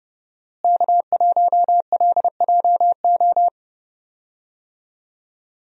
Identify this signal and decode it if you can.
Morse code, heard as a keyed tone, about 31 wpm, 700 Hz. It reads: K1LJO
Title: USA Call Signs - 5 Characters - Repeated 2x - 30wpm